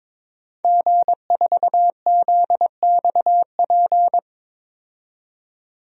Morse code, heard as a keyed tone, about 22 wpm, 700 Hz. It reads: G4ZXP
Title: DXCC-100 Call Signs - 5 Characters - 22wpm